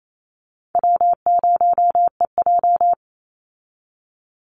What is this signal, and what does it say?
Morse code, heard as a keyed tone, about 28 wpm, 700 Hz. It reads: W0EJ
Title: Easy Call Signs - 28wpm